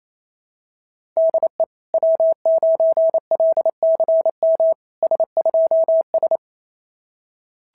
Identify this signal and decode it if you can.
Morse code, heard as a keyed tone, about 27 wpm, 650 Hz. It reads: DE W9LCM S2S
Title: SOTA QSO Practice 28wpm